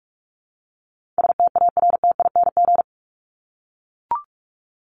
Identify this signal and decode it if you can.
Morse code, heard as a keyed tone, about 44 wpm, 700 Hz. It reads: STARTING
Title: ING Suffix - Top 100 Words with Base Words 45wpm